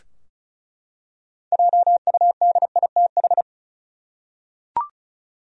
Morse code, heard as a keyed tone, about 34 wpm, 700 Hz. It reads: JUDITH
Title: USA Common Names - 35wpm